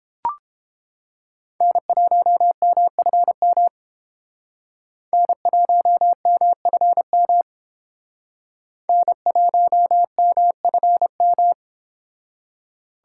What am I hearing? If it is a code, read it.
Morse code, heard as a keyed tone, about 30 wpm, 700 Hz. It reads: N1MFM N1MFM N1MFM